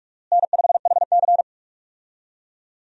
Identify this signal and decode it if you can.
Morse code, heard as a keyed tone, about 46 wpm, 700 Hz. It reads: N5HC
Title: USA Call Signs - 4 Characters - Repeated 3x - 45wpm